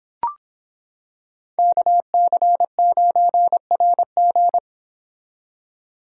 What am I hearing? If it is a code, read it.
Morse code, heard as a keyed tone, about 26 wpm, 700 Hz. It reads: KC9RG